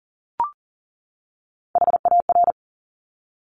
Morse code, heard as a keyed tone, about 38 wpm, 690 Hz. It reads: HAR
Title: Top Three Letter Combinations in Words 40wpm